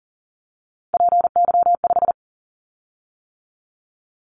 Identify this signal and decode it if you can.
Morse code, heard as a keyed tone, about 39 wpm, 700 Hz. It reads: PY5
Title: Callsign Prefixes for Top 100 DXCC - 40wpm